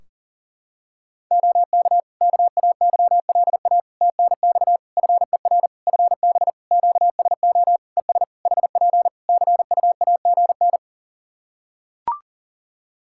Morse code, heard as a keyed tone, about 40 wpm, 700 Hz. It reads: OK KAYLA TNX FER FB QSO ES HP CUAGN
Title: K1USN SST QSO Practice 40wpm